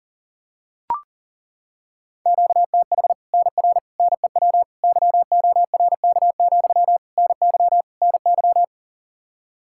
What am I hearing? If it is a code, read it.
Morse code, heard as a keyed tone, about 37 wpm, 700 Hz. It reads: QTH NR NEW YORK, NY NY
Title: Practice QSO - Exchange 1 - Beginning - 40wpm